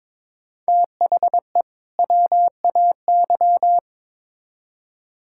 Morse code, heard as a keyed tone, about 21 wpm, 700 Hz. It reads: THE WAY
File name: Build A Phrase - 2-Word Phrases - 22wpm